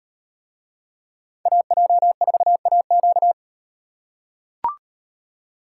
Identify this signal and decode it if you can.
Morse code, heard as a keyed tone, about 38 wpm, 700 Hz. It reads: AJ4AQ